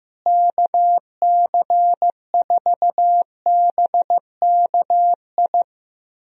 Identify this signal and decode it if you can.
Morse code, heard as a keyed tone, about 15 wpm, 700 Hz. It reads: KC4BKI
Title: USA Call Signs - 6 Characters - Repeated 3x - 15wpm